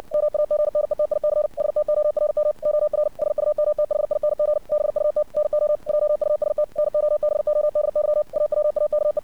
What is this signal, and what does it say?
Morse code, heard as a keyed tone, about 60 wpm, 600 Hz. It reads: MAGNETIC STORM OR SUMTHING BUT NO JUST NOBODY AWAKE
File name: archived / CRSnet